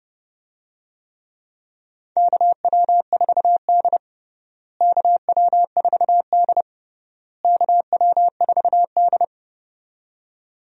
Morse code, heard as a keyed tone, about 28 wpm, 700 Hz. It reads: KW4D KW4D KW4D
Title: USA Call Signs - 4 Characters - Repeated 3x - 30wpm